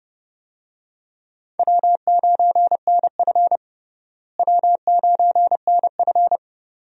Morse code, heard as a keyed tone, about 30 wpm, 700 Hz. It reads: W9NF W9NF
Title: USA Call Signs - 4 Characters - Repeated 2x - 30wpm